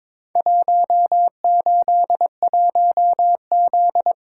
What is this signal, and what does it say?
Morse code, heard as a keyed tone, about 23 wpm, 700 Hz. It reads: 181Z
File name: practice / 22wpm / Four Letter-Number Combinations - 22wpm